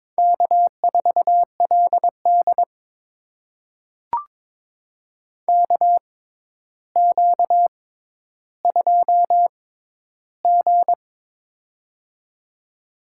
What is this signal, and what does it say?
Morse code, heard as a keyed tone, about 21 wpm, spaced as for 11 wpm, 700 Hz. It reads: K4LD KQ2G
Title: USA Call Signs - 4 Characters - 6x Character Spacing - 22wpm